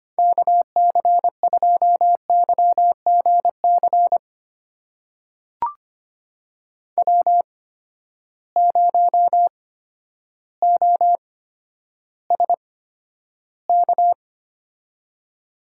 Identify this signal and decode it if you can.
Morse code, heard as a keyed tone, about 25 wpm, spaced as for 11 wpm, 700 Hz. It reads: KC2YGC W0OSK
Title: USA Call Signs - 8x Character Spacing - 25wpm